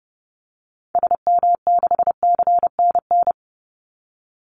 Morse code, heard as a keyed tone, about 30 wpm, 700 Hz. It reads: SM6CNN
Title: Call Signs - 30wpm